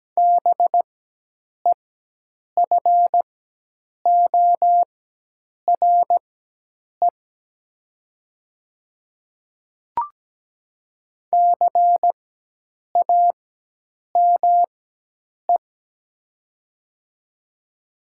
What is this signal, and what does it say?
Morse code, heard as a keyed tone, about 17 wpm, spaced as for 8 wpm, 700 Hz. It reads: BEFORE CAME
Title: 4x Character Spacing - Top 200 Words - 17wpm